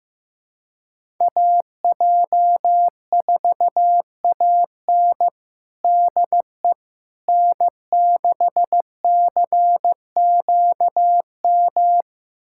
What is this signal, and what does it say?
Morse code, heard as a keyed tone, about 16 wpm, 700 Hz. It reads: AJ4AN DE N6CQM
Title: USA Callsign DE Callsign - 5 Characters - 15wpm